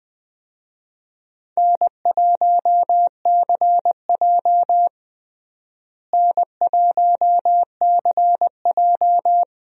Morse code, heard as a keyed tone, about 20 wpm, 700 Hz. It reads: N1CJ N1CJ
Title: USA Call Signs - 4 Characters - Repeated 2x - 20wpm